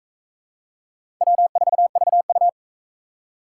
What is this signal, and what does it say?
Morse code, heard as a keyed tone, about 44 wpm, 700 Hz. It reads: W4VU